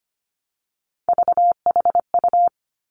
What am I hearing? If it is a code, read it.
Morse code, heard as a keyed tone, about 25 wpm, 700 Hz. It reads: VHU
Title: Three Challenging Letter-Number Combinations 25wpm